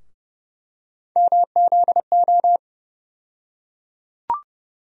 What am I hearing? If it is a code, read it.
Morse code, heard as a keyed tone, about 30 wpm, 700 Hz. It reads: MZO
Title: USA Call Sign Postfixes - 3 Letters 30wpm